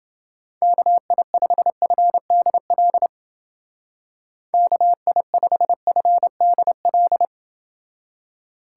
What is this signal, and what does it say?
Morse code, heard as a keyed tone, about 30 wpm, 700 Hz. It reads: KI5FDL KI5FDL